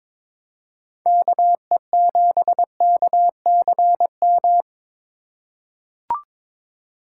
Morse code, heard as a keyed tone, about 22 wpm, 700 Hz. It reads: KE7KCM
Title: USA Call Signs - 3x Character Spacing - 22wpm